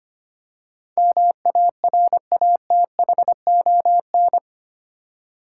Morse code, heard as a keyed tone, about 25 wpm, 700 Hz. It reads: MARATHON
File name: Eight Letter Words - 25wpm